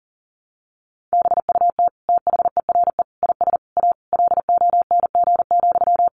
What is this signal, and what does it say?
Morse code, heard as a keyed tone, about 40 wpm, 700 Hz. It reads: BUT THERE IS A LONG,
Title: Sentences from Top 1100 Words 40wpm